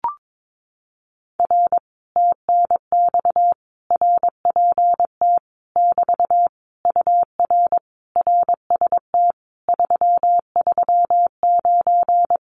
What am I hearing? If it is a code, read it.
Morse code, heard as a keyed tone, about 22 wpm, 700 Hz. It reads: R TNX RPT <BT> UR RST 339
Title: Common QSO Phrases Encore 22wpm